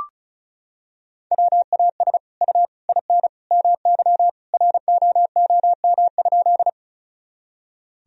Morse code, heard as a keyed tone, about 34 wpm, 700 Hz. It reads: WAS U IN MY ROOM?
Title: Sentences from Top 300 Words and Abbreviations 35wpm